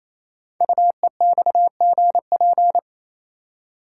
Morse code, heard as a keyed tone, about 27 wpm, 700 Hz. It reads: UEXGP
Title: Code Groups - 5 Letters - 28wpm